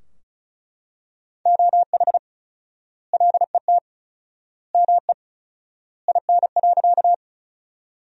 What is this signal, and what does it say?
Morse code, heard as a keyed tone, about 36 wpm, 700 Hz. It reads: OH LET ME IN.
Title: 4x Word-Spacing - Sentences from Top 500 Words - 35wpm